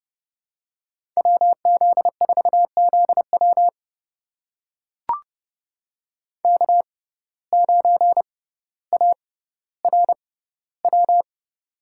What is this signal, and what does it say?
Morse code, heard as a keyed tone, about 31 wpm, spaced as for 15 wpm, 700 Hz. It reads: WZ4ZW K9ARW
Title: USA Call Signs - 5 Characters - 6x Character Spacing - 30wpm